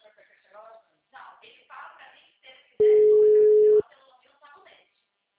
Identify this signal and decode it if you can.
Ringback tone (Europe, 425 Hz), with speech.